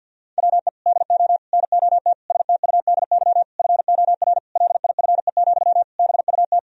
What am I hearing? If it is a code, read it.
Morse code, heard as a keyed tone, about 52 wpm, 700 Hz. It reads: WE DO NOT STUDY FOR LIFE, BUT
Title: Sentences from Top 1000 Words - Review Part 1 - 50wpm